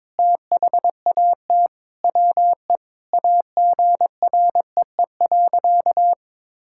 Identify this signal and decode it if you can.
Morse code, heard as a keyed tone, about 22 wpm, 700 Hz. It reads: THAT WE AGREE.
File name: practice / 22wpm / Sentences with 8-Word Phrases - 22wpm